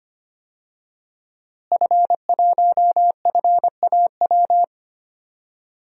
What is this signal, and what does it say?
Morse code, heard as a keyed tone, about 25 wpm, 700 Hz. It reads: F1FAW
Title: DXCC-100 Call Signs - 5 Characters - Repeated 3x - 25wpm